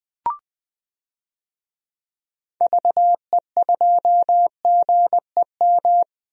Morse code, heard as a keyed tone, about 21 wpm, 700 Hz. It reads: VE2GEM
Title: DXCC-100 Call Signs - 6 Characters - 20wpm